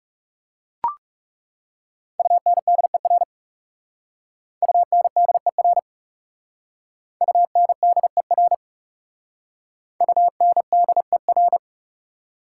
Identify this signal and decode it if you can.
Morse code, heard as a keyed tone, about 36 wpm, 700 Hz. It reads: UNDER UNDER UNDER UNDER
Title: Speed-Racing Prefixes and Suffixes 30wpm